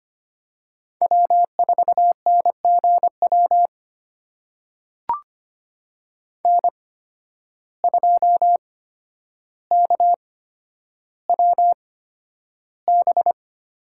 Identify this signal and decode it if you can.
Morse code, heard as a keyed tone, about 25 wpm, spaced as for 11 wpm, 700 Hz. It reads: W4NGW N2KWB
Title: USA Call Signs - 5 Characters - 8x Character Spacing - 25wpm